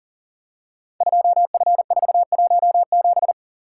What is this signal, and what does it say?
Morse code, heard as a keyed tone, about 42 wpm, 700 Hz. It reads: 2F417
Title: Code Groups - 5 Letter:Numbers - 40wpm